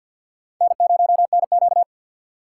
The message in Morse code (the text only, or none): N0NQ